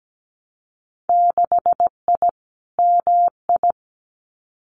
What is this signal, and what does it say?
Morse code, heard as a keyed tone, about 17 wpm, 700 Hz. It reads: 6I MI
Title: Winter Field Day Exchanges Encore 17wpm